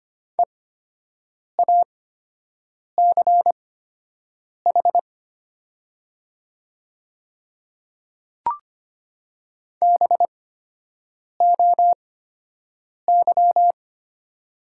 Morse code, heard as a keyed tone, about 25 wpm, spaced as for 7 wpm, 700 Hz. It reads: EACH BOY
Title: 8x Character Spacing - Top 300 Words - 25wpm